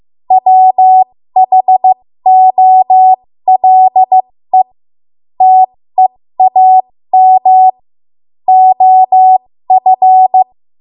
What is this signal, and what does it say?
Morse code, heard as a keyed tone, about 15 wpm, spaced as for 13 wpm, 750 Hz. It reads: WHOLE TEAM OF
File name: Morse / 13WPM